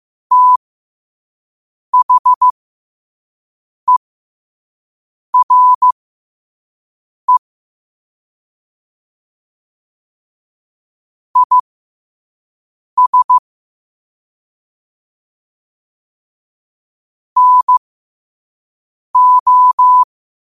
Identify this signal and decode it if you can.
Morse code, heard as a keyed tone, about 15 wpm, spaced as for 5 wpm, 1000 Hz. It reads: THERE IS NO
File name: qotdfiles / 5wpm